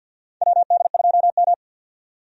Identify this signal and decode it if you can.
Morse code, heard as a keyed tone, about 48 wpm, 700 Hz. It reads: WD2M
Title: USA Call Signs - 4 Characters - 50wpm